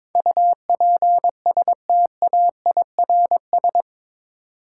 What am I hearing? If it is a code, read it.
Morse code, heard as a keyed tone, about 22 wpm, 700 Hz. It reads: UPSTAIRS
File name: practice / 22wpm / Eight Letter Words - 22wpm